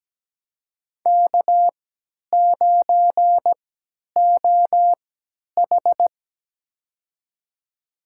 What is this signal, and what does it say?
Morse code, heard as a keyed tone, about 17 wpm, spaced as for 10 wpm, 700 Hz. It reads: K9OH
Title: USA Call Signs - 4 Characters - 3x Character Spacing - 17wpm